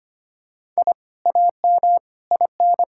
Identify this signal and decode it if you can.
Morse code, heard as a keyed tone, about 25 wpm, 700 Hz. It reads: I AM IN
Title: Sentences from Top 1000 Words - Review Part 6 - 25wpm